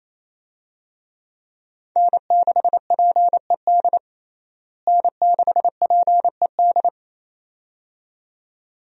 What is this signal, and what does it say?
Morse code, heard as a keyed tone, about 27 wpm, 700 Hz. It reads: N6PED N6PED
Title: USA Call Signs - 5 Characters - Repeated 2x - 28wpm